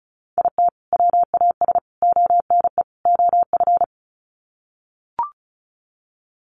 Morse code, heard as a keyed tone, about 34 wpm, 700 Hz. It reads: IT WAS ONE OF
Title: Sets of 4 Words 35wpm